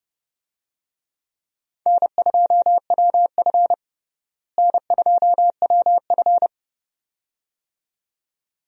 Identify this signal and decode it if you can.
Morse code, heard as a keyed tone, about 30 wpm, 700 Hz. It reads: N2WF N2WF
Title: USA Call Signs - 4 Characters - Repeated 2x - 30wpm